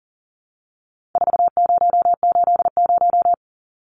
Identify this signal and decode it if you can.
Morse code, heard as a keyed tone, about 40 wpm, 700 Hz. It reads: 4080
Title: Numbers - 4-Digit 40wpm